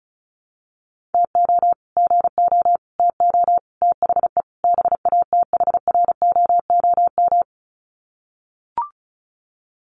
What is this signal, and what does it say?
Morse code, heard as a keyed tone, about 34 wpm, 700 Hz. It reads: TO GO TO THE BATHROOM
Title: Sets of 5 Words Encore 35wpm